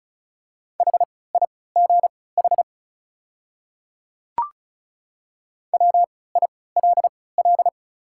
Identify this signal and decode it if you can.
Morse code, heard as a keyed tone, about 36 wpm, spaced as for 20 wpm, 700 Hz. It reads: HIGH WILL
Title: 3x Character Spacing - Top 200 Words - 35wpm